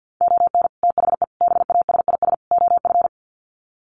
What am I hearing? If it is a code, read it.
Morse code, heard as a keyed tone, about 50 wpm, 700 Hz. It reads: ON THE BASIS OF
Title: Sets of 4 Words 50wpm